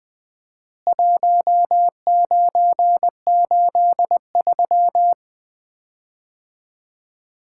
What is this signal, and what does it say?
Morse code, heard as a keyed tone, about 20 wpm, 700 Hz. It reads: 1983
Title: Common QSO Phrases - 4x Word-Spacing - 20wpm